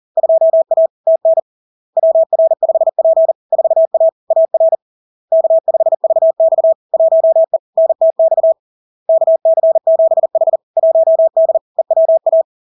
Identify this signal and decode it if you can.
Morse code, heard as a keyed tone, about 40 wpm, 650 Hz. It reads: 2A TN WR5P 4A AR K5VX 1E NTX KC7H 1D EWA
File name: LARK-FD-2022-Log-in-CW-40wpm-650-Hz